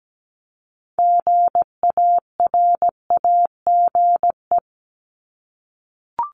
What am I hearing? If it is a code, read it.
Morse code, heard as a keyed tone, about 17 wpm, 700 Hz. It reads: GARAGE
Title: Multi-letter Phonemes Repeated 3x 17wpm